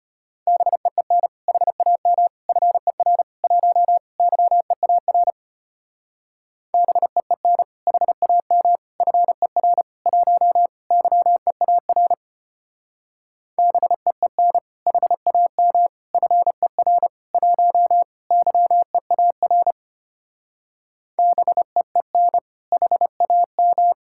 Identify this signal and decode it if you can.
Morse code, heard as a keyed tone, about 32 wpm, 700 Hz. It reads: BEEN HAM FER 1 YEAR BEEN HAM FER 1 YEAR BEEN HAM FER 1 YEAR BEEN HAM